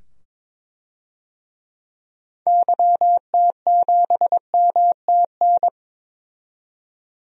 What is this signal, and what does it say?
Morse code, heard as a keyed tone, about 22 wpm, 700 Hz. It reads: YT7MTN
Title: DXCC-100 Call Signs - 6 Characters - 22wpm